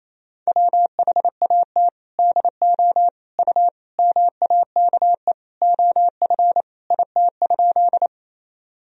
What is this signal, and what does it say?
Morse code, heard as a keyed tone, about 27 wpm, 700 Hz. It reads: WHAT DO U MAKE OF IT?
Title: Sentences from Top 100 Words and Abbreviations 28wpm